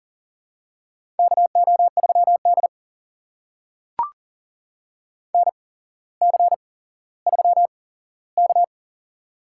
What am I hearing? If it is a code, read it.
Morse code, heard as a keyed tone, about 40 wpm, spaced as for 17 wpm, 700 Hz. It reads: KO3D NC3K
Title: USA Call Signs - 4 Characters - 8x Character Spacing - 40wpm